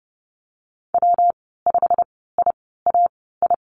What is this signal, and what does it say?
Morse code, heard as a keyed tone, about 30 wpm, spaced as for 17 wpm, 700 Hz. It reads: W5IAI
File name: Lesson 19g - Letter w - Call Signs 30wpm